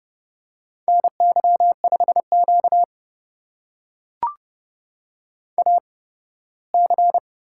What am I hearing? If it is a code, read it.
Morse code, heard as a keyed tone, about 31 wpm, spaced as for 18 wpm, 700 Hz. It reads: NY5Q AC
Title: USA Call Signs - 8x Character Spacing - 30wpm